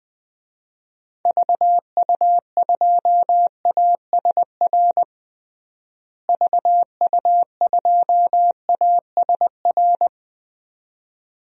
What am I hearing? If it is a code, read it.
Morse code, heard as a keyed tone, about 20 wpm, 700 Hz. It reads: VU2ASR VU2ASR
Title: DXCC-100 Call Signs - 6 Characters - Repeated 2x - 20wpm